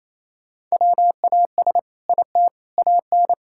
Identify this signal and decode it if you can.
Morse code, heard as a keyed tone, about 27 wpm, 700 Hz. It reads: WAS IT AN
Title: Sentences from Top 1000 Words - Review Part 5 - 28wpm